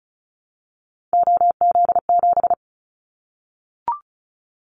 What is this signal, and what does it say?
Morse code, heard as a keyed tone, about 34 wpm, 700 Hz. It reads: OZ7
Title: Callsign Prefixes for Top 100 DXCC - 35wpm